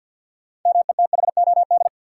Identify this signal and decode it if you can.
Morse code, heard as a keyed tone, about 51 wpm, 700 Hz. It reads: METHOD
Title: Sentences from Top 600 Words - New Word Drill - 50wpm